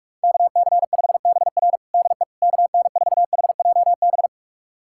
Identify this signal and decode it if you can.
Morse code, heard as a keyed tone, about 46 wpm, 700 Hz. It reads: KC5BR DE KN4HJB
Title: USA Callsign DE Callsign 45wpm